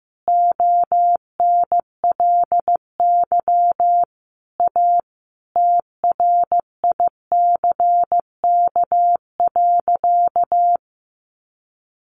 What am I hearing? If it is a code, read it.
Morse code, heard as a keyed tone, about 15 wpm, 700 Hz. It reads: ONLY A TRICK.
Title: Sentences from Top 1500 Words 15wpm